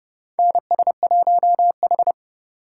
Morse code, heard as a keyed tone, about 30 wpm, 700 Hz. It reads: NS1H
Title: USA Call Signs - 4 Characters - 2x Character Spacing - 30wpm